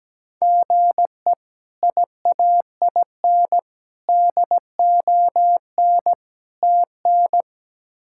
Morse code, heard as a keyed tone, about 17 wpm, 700 Hz. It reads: GE IAIN DON TN
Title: K1USN SST QSO Practice - 17wpm